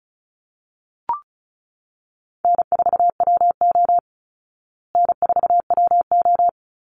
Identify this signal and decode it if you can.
Morse code, heard as a keyed tone, about 34 wpm, 700 Hz. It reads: N4WO N4WO
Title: Easy Call Signs Repeated 3x - 35wpm